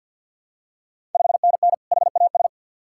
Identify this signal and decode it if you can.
Morse code, heard as a keyed tone, about 50 wpm, 700 Hz. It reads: 5NN HAS
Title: Ontario QSO Party Exchange 50wpm